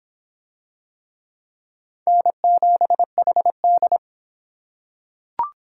Morse code, heard as a keyed tone, about 25 wpm, 700 Hz. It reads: N7HD